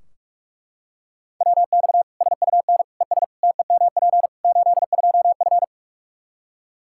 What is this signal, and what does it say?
Morse code, heard as a keyed tone, about 41 wpm, 700 Hz. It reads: WX SUN ES TEMP 82F
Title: K1USN SST QSO Practice 45wpm